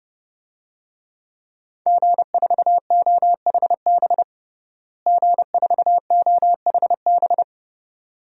Morse code, heard as a keyed tone, about 30 wpm, 700 Hz. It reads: G4OHB G4OHB
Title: DXCC-100 Call Signs - 5 Characters - Repeated 2x - 30wpm